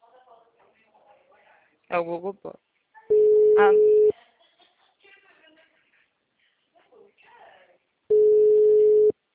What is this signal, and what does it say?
Ringback tone (Europe, 425 Hz), with speech.
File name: out- -2006- - - .370